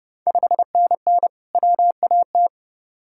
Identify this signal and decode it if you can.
Morse code, heard as a keyed tone, about 30 wpm, 700 Hz. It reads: 5NN WAT
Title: Ontario QSO Party Exchange 30wpm